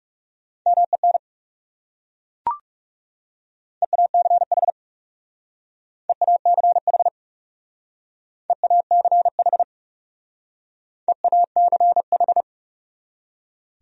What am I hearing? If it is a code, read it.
Morse code, heard as a keyed tone, about 37 wpm, 700 Hz. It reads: MEN EACH EACH EACH EACH
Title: Speed-Racing Top 500 Words 30wpm